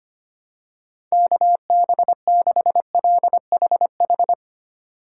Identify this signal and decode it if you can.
Morse code, heard as a keyed tone, about 25 wpm, 700 Hz. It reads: KB6LHH
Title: USA Call Signs - 3x Character Spacing - 25wpm